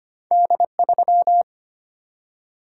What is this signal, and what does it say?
Morse code, heard as a keyed tone, about 25 wpm, 700 Hz. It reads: D3
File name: Two Letter-Number Combinations - 25wpm